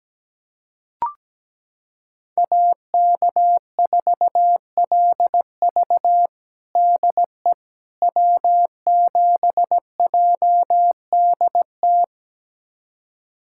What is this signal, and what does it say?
Morse code, heard as a keyed tone, about 17 wpm, 700 Hz. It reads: AK4LV DE W7JDT
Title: USA Callsign DE Callsign - 5 Characters - 17wpm